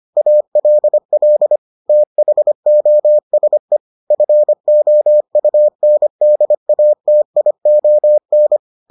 Morse code, heard as a keyed tone, about 25 wpm, 600 Hz. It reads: ALL THOSE FOUNDATION